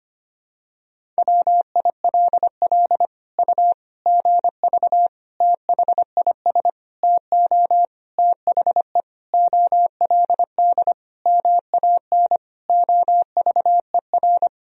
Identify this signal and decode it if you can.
Morse code, heard as a keyed tone, about 25 wpm, 700 Hz. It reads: WILL U GV THIS TO THE OLD MAN OVER
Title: Sentences from Top 200 Words and Abbreviations 25wpm